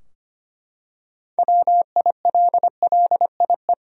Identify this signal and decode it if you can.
Morse code, heard as a keyed tone, about 25 wpm, 700 Hz. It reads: WILLIE
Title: USA Common Names - 25wpm